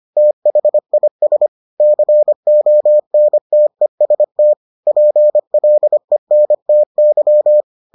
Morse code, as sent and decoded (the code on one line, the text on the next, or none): - .... .. ... / -.-. --- -. - . ... - / .--. .-.. -. - -.--
THIS CONTEST PLENTY